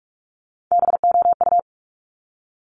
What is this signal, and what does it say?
Morse code, heard as a keyed tone, about 46 wpm, 700 Hz. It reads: BOU
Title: Top Three Letter Combinations in Words 45wpm